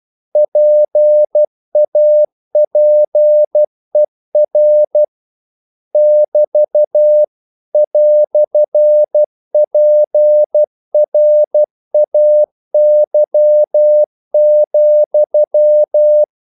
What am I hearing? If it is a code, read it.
Morse code, heard as a keyed tone, about 12 wpm, 600 Hz. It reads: PAPER <BT> "PRAY,
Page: English quotations in CW (telegraphy - morse)